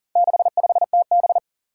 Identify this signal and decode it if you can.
Morse code, heard as a keyed tone, about 40 wpm, 700 Hz. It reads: 65TB
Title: Four Challenging Letter-Number Combinations 40wpm